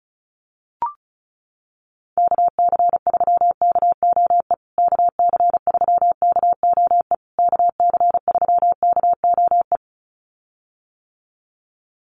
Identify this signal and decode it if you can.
Morse code, heard as a keyed tone, about 34 wpm, 700 Hz. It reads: KC3KOE KC3KOE KC3KOE
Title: USA Call Signs Repeated 3x 35wpm